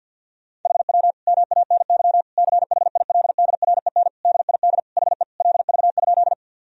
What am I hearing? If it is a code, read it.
Morse code, heard as a keyed tone, about 49 wpm, 700 Hz. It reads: HW MANY CHILDREN DID HE LV?
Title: Sentences from Top 400 Words and Abbreviations 50wpm